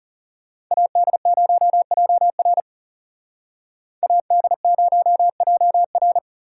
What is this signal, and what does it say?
Morse code, heard as a keyed tone, about 34 wpm, 700 Hz. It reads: AD0JR AD0JR
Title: Speed Racing - USA 2x2 Call Signs 30wpm